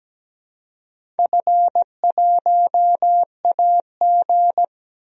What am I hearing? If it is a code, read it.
Morse code, heard as a keyed tone, about 17 wpm, 700 Hz. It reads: F1AG
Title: DXCC-100 Call Signs - 4 Characters - 17wpm